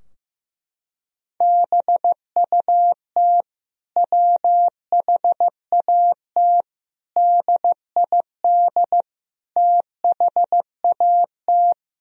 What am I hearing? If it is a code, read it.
Morse code, heard as a keyed tone, about 15 wpm, 700 Hz. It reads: BUT WHAT DID THAT
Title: Sentences from Top 1000 Words - Review Part 4 - 15wpm